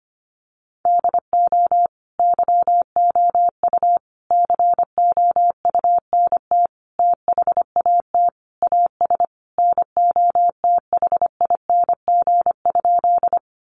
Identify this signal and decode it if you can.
Morse code, heard as a keyed tone, about 25 wpm, 700 Hz. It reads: DO YOU COUNT THAT AS NOTHING?
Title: Sentences from Top 1500 Words 25wpm